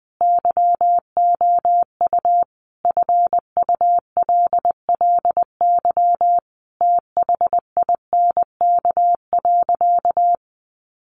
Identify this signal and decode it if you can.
Morse code, heard as a keyed tone, about 20 wpm, 700 Hz. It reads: YOU FULLY THINK.
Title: Sentences from Top 1300 Words 20wpm